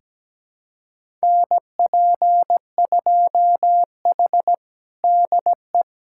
Morse code, heard as a keyed tone, about 17 wpm, 700 Hz. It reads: NP2H DE
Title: USA Callsign DE Callsign - 4 Characters - 17wpm